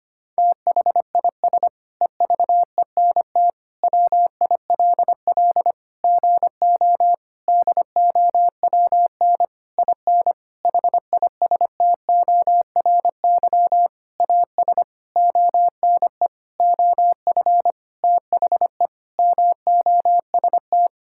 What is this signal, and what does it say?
Morse code, heard as a keyed tone, about 25 wpm, 700 Hz. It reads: THIS EVENT WILL GO DOWN IN HISTORY AS ONE OF THE MOST
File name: practice / 25wpm / Sentences with 9-Word Phrases - 25wpm